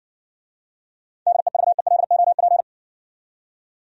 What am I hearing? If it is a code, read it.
Morse code, heard as a keyed tone, about 56 wpm, 700 Hz. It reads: DEVELOP